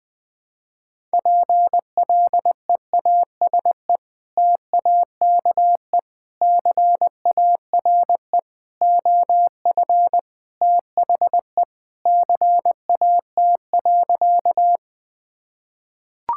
Morse code, heard as a keyed tone, about 20 wpm, 700 Hz. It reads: PLEASE TAKE CARE OF THE CAT.
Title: Sentences with 3-Word Phrases - Spoken Beginning - 20wpm